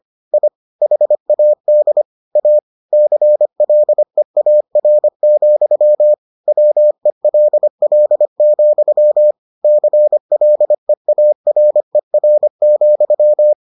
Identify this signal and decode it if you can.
Morse code, heard as a keyed tone, about 25 wpm, 600 Hz. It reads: I HAD A CLEAR, WELL, CLEARER,